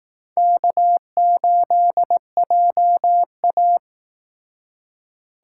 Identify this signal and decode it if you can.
Morse code, heard as a keyed tone, about 18 wpm, 700 Hz. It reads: K8JA
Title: Speed Racing - USA 1x2 Call Signs 15wpm